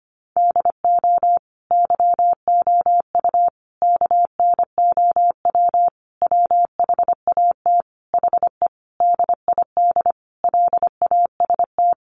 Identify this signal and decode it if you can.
Morse code, heard as a keyed tone, about 25 wpm, 700 Hz. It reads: DO YOU KNOW WHAT HE DID LAST
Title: Sentences from Top 300 Words 25wpm